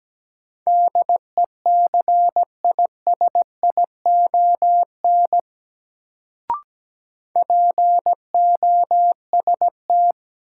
Morse code, heard as a keyed tone, about 17 wpm, 700 Hz. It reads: DECISION POST
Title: Sentences from Top 1000 Words - New Word Drill - 17wpm